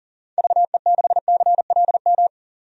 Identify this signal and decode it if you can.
Morse code, heard as a keyed tone, about 37 wpm, 700 Hz. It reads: VE6CLM
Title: DXCC-100 Call Signs - 6 Characters - 40wpm